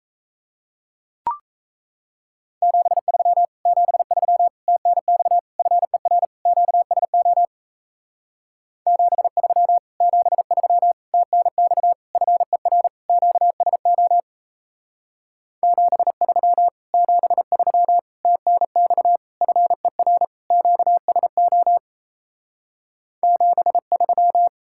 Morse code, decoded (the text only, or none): 73 73 TNX FER QSO 73 73 TNX FER QSO 73 73 TNX FER QSO 73